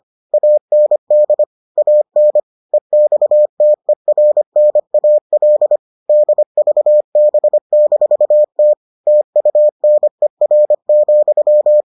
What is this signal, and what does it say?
Morse code, heard as a keyed tone, about 25 wpm, 600 Hz. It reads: AND AN EXTERNAL DVB-T TUNER,